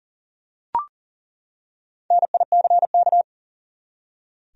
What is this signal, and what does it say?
Morse code, heard as a keyed tone, about 40 wpm, 700 Hz. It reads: NICK
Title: USA Common Names - 40wpm